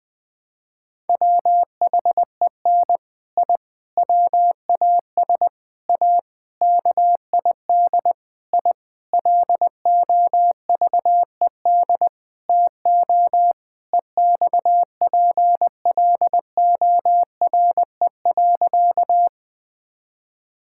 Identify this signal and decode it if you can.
Morse code, heard as a keyed tone, about 21 wpm, 700 Hz. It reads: WHEN I WAS A KID I LOVED TO EXPLORE.
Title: Sentences with 6-Word Phrases - 20wpm